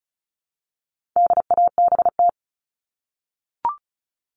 Morse code, heard as a keyed tone, about 34 wpm, 700 Hz. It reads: DABT
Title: All Cut Numbers - 4-Digit 35wpm